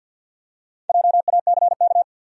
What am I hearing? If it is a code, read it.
Morse code, heard as a keyed tone, about 52 wpm, 700 Hz. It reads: JACK
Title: USA Common Names - Max 5 Letters - 50wpm